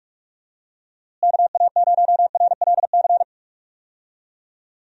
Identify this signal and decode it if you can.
Morse code, heard as a keyed tone, about 46 wpm, 700 Hz. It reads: KA0RLC
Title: USA Call Signs - 6 Characters - Repeated 3x - 45wpm